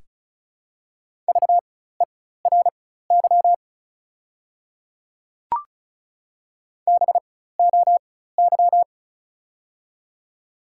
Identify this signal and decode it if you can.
Morse code, heard as a keyed tone, about 36 wpm, spaced as for 16 wpm, 700 Hz. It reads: VERY BOY
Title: 4x Character Spacing - Top 200 Words - 35wpm